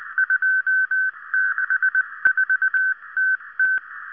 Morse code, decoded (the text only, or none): 264TT